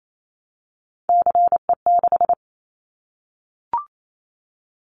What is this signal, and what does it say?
Morse code, heard as a keyed tone, about 28 wpm, 700 Hz. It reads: CE6
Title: Callsign Prefixes for Top 100 DXCC - 28wpm